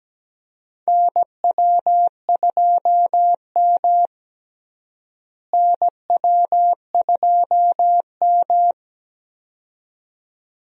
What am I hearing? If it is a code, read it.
Morse code, heard as a keyed tone, about 17 wpm, 700 Hz. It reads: NW2M NW2M
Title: USA Call Signs - 4 Characters - Repeated 2x - 17wpm